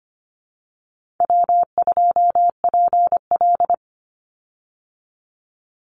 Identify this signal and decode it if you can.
Morse code, heard as a keyed tone, about 25 wpm, 700 Hz. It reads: W2PL
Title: Easy Call Signs Repeated 3x - 25wpm